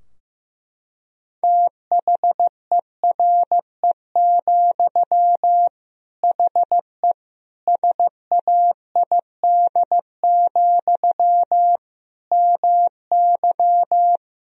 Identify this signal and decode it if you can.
Morse code, heard as a keyed tone, about 16 wpm, 700 Hz. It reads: THERE, HE SAID, MY
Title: Sentences from Top 1000 Words - Review Part 2 - 15wpm